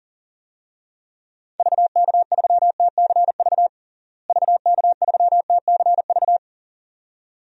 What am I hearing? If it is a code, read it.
Morse code, heard as a keyed tone, about 40 wpm, 700 Hz. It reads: VK3TCV VK3TCV
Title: DXCC-100 Call Signs - 6 Characters - Repeated 2x - 40wpm